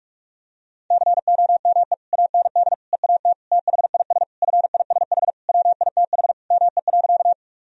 Morse code, heard as a keyed tone, about 46 wpm, 700 Hz. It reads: COME AND EAT THIS FISH WITH ME.
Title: Sentences from Top 1000 Words - Review Part 1 - 45wpm